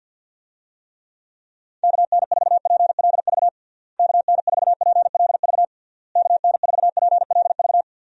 Morse code, heard as a keyed tone, about 51 wpm, 700 Hz. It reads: KN4PLV KN4PLV KN4PLV
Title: USA Call Signs - 6 Characters - Repeated 3x - 50wpm